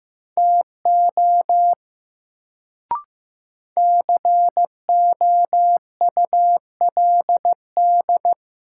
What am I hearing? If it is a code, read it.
Morse code, heard as a keyed tone, about 15 wpm, 700 Hz. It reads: TO COULD
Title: Sentences from Top 100 Words - New Word Drill - 15wpm